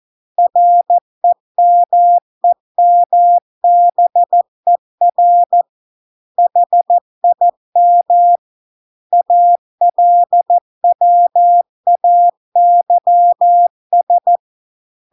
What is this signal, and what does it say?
Morse code, heard as a keyed tone, about 14 wpm, 700 Hz. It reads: REMEMBER HIM ALWAYS